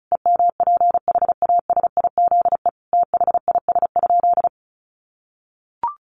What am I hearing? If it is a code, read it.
Morse code, heard as a keyed tone, about 34 wpm, 700 Hz. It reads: EMPHASIZE THIS?
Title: Sentences from Top 1500 Words 35wpm